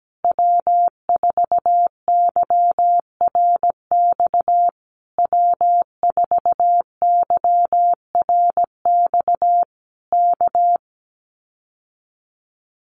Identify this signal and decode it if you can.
Morse code, heard as a keyed tone, about 17 wpm, 700 Hz. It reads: W4YRX W4YRX K
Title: Common QSO Phrases Encore 17wpm